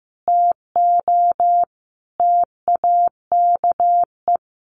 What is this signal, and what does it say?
Morse code, heard as a keyed tone, about 15 wpm, 700 Hz. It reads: TO TAKE
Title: Sentences from Top 1500 Words 15wpm